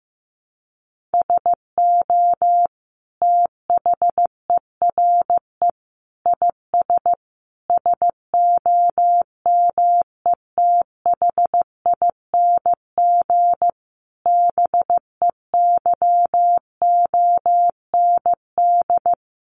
Morse code, heard as a keyed tone, about 14 wpm, 700 Hz. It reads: SO THERE IS SOMETHING BEYOND
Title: Sentences from Top 1500 Words 15wpm